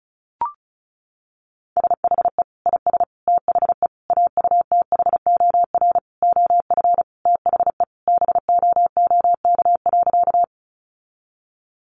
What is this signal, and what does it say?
Morse code, heard as a keyed tone, about 34 wpm, 700 Hz. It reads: SHE IS THE AUTHOR OF THE BOOK.
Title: Sentences from Top 1200 Words 35wpm